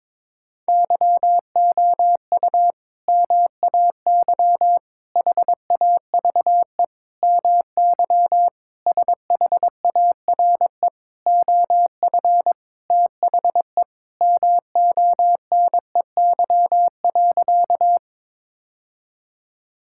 Morse code, heard as a keyed tone, about 22 wpm, 700 Hz. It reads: YOU MAY HAVE MY SHARE OF THE MONEY.
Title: Sentences from Top 1000 Words - Review Part 6 - 22wpm